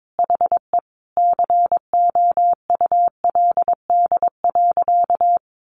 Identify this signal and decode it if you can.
Morse code, heard as a keyed tone, about 22 wpm, 700 Hz. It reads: HE COULD.
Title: Sentences from Top 500 Words 22wpm